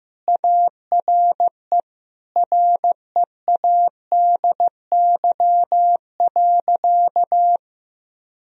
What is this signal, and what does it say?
Morse code, heard as a keyed tone, about 16 wpm, 700 Hz. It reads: ARE READY.
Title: Sentences from Top 1000 Words - Review Part 1 - 15wpm